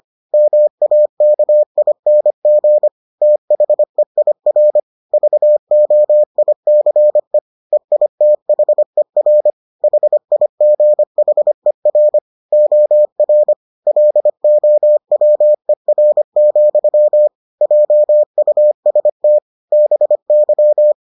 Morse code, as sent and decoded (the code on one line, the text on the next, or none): -- .- -.- .. -. --. / - .... .. .-. / ...- --- .. -.-. / . .. - .... .-. / .... .. --. .... .-. / --- .-. / .-.. --- .-- . .-. --..-- / .--- ..- ... - / -... -.--
MAKING THEIR VOICE EITHER HIGHER OR LOWER, JUST BY